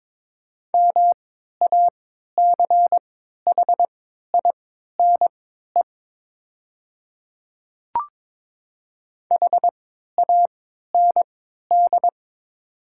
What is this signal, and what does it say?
Morse code, heard as a keyed tone, about 23 wpm, spaced as for 13 wpm, 700 Hz. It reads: MACHINE HAND
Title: 3x Character Spacing - Top 500 Words - 22wpm